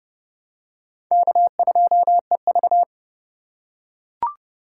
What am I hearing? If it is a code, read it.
Morse code, heard as a keyed tone, about 27 wpm, 700 Hz. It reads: K2EV